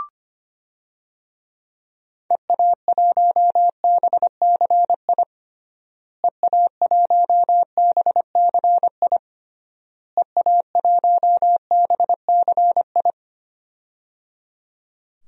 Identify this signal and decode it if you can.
Morse code, heard as a keyed tone, about 25 wpm, 700 Hz. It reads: EA1BCI EA1BCI EA1BCI
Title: DXCC-100 Call Signs - 6 Characters - Repeated 3x - 25wpm